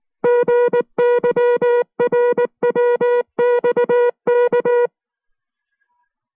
Morse code, heard as a keyed tone, about 19 wpm, 480 Hz. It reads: GYRWXK